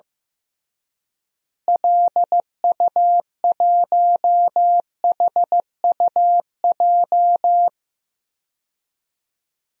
Morse code, heard as a keyed tone, about 16 wpm, 700 Hz. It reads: LU1HUJ
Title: DXCC-100 Call Signs - 6 Characters - Repeated 3x - 15wpm